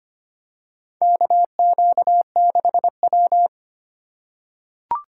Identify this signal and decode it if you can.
Morse code, heard as a keyed tone, about 25 wpm, 700 Hz. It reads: KQ6W
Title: USA Call Signs - 4 Characters - 2x Character Spacing - 25wpm